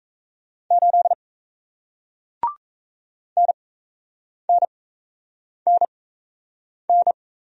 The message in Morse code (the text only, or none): 8 NNNN